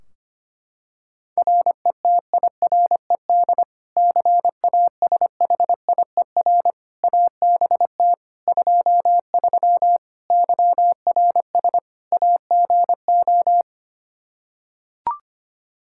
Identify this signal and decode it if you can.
Morse code, heard as a keyed tone, about 25 wpm, 700 Hz. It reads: RETIRED CASHIER ABT 23 YRS AGO
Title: Practice QSO - Exchange 3 - More Info - 25wpm